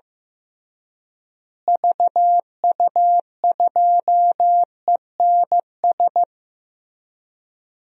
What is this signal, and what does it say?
Morse code, heard as a keyed tone, about 15 wpm, 700 Hz. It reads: VU2ENS
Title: DXCC-100 Call Signs - 6 Characters - Repeated 3x - 15wpm